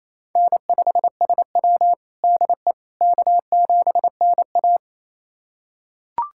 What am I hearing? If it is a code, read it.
Morse code, heard as a keyed tone, about 28 wpm, 700 Hz. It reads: N5SW DE K7NA
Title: USA Callsign DE Callsign - 4 Characters - 28wpm